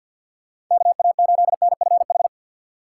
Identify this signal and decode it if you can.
Morse code, heard as a keyed tone, about 52 wpm, 700 Hz. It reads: KA8NFH
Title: USA Call Signs - 6 Characters - 6x Character Spacing - 50wpm